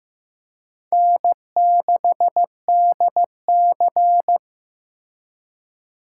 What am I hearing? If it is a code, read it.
Morse code, heard as a keyed tone, about 18 wpm, 700 Hz. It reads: N6DC
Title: Speed Racing - USA 1x2 Call Signs 15wpm